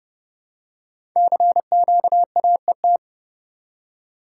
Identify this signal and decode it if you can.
Morse code, heard as a keyed tone, about 28 wpm, 700 Hz. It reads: CQAET
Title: Code Groups - 5 Letter:Numbers - 30wpm